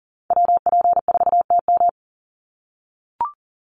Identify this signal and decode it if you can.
Morse code, heard as a keyed tone, about 40 wpm, 700 Hz. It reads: WP4TM
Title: USA Call Signs 40wpm